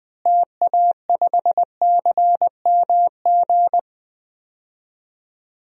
Morse code, heard as a keyed tone, about 20 wpm, 700 Hz. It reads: TA5CMG
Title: DXCC-100 Call Signs - 6 Characters - Repeated 2x - 20wpm